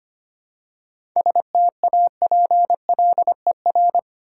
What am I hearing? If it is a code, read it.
Morse code, heard as a keyed tone, about 25 wpm, 700 Hz. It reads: STAPLER
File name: practice / 25wpm / Easy Words with Hints - 25wpm